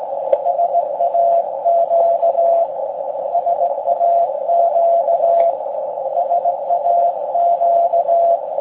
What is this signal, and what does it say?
Morse code, heard as a keyed tone, about 18 wpm, 690 Hz, with noise. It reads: SAQ SAQ SAQ
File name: forum / im / aq grimeton